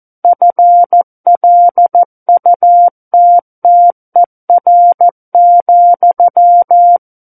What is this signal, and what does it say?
Morse code, heard as a keyed tone, about 14 wpm, 700 Hz. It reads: FLUTTER,
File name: Book One / 05-Chapter 5 - 14 WPM